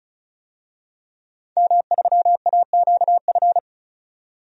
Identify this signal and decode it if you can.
Morse code, heard as a keyed tone, about 34 wpm, 700 Hz. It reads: M3AQF
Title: DXCC-100 Call Signs - 5 Characters - 35wpm